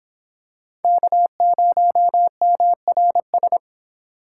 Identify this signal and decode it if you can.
Morse code, heard as a keyed tone, about 26 wpm, 700 Hz. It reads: K0MRS